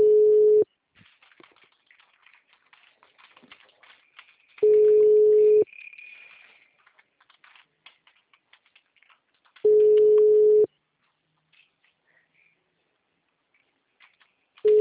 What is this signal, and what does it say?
Ringback tone (Europe, 425 Hz).